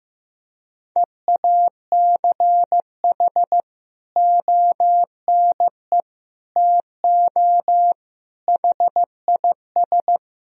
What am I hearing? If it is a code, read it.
Morse code, heard as a keyed tone, about 15 wpm, 700 Hz. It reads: EACH ONE TO HIS
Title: Sentences from Top 1000 Words - Review Part 4 - 15wpm